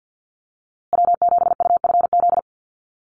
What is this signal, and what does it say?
Morse code, heard as a keyed tone, about 48 wpm, 700 Hz. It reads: W7UFZ
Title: USA Call Signs Repeated 3x 50wpm